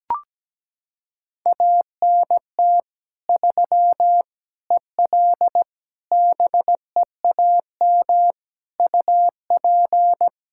Morse code, heard as a keyed tone, about 17 wpm, 700 Hz. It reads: ANT 3 EL BEAM UP
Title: Practice QSO - Exchange 2 - Info - 17wpm